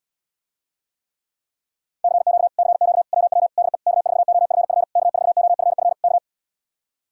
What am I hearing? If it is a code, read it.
Morse code, heard as a keyed tone, about 80 wpm, 700 Hz. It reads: CQ CQ WW DE K4OFF K4OFF K